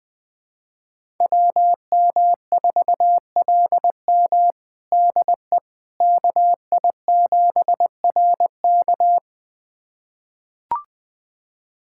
Morse code, heard as a keyed tone, about 20 wpm, 700 Hz. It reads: WM4LM DE KI7RK
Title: USA Callsign DE Callsign - 5 Characters - 20wpm